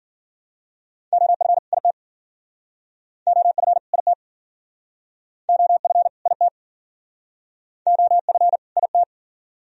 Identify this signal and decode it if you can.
Morse code, heard as a keyed tone, about 53 wpm, 700 Hz. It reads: OF IT OF IT OF IT OF IT